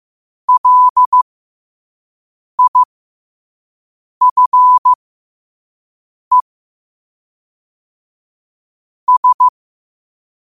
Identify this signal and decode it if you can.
Morse code, heard as a keyed tone, about 15 wpm, spaced as for 5 wpm, 1000 Hz. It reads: LIFE S
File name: qotdfiles / 5wpm